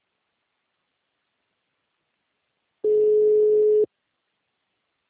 Ringback tone (Europe, 425 Hz).